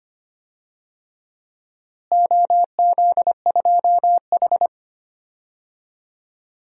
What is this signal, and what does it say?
Morse code, heard as a keyed tone, about 25 wpm, 700 Hz. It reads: OZ2H
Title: DXCC-100 Call Signs - 4 Characters - Repeated 3x - 25wpm